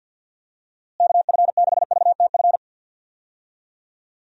Morse code, heard as a keyed tone, about 51 wpm, 700 Hz. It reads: KF6VTF
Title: USA Call Signs - 6 Characters - Repeated 3x - 50wpm